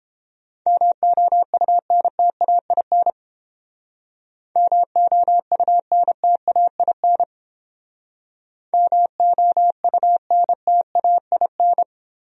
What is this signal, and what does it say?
Morse code, heard as a keyed tone, about 30 wpm, 700 Hz. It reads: MOUNTAIN MOUNTAIN MOUNTAIN